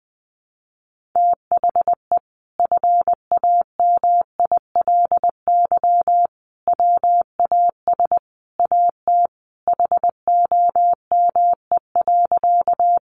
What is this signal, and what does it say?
Morse code, heard as a keyed tone, about 20 wpm, 700 Hz. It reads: THE FAMILY WAS AT HOME.
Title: Sentences from Top 400 Words 20wpm